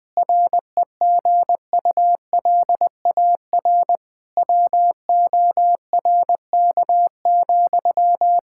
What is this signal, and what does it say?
Morse code, heard as a keyed tone, about 21 wpm, 700 Hz. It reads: REGULAR WORK,
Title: Sentences from Top 1000 Words - Review Part 6 - 20wpm